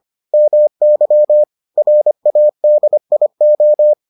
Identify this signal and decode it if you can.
Morse code, heard as a keyed tone, about 25 wpm, 600 Hz. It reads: MY RADIO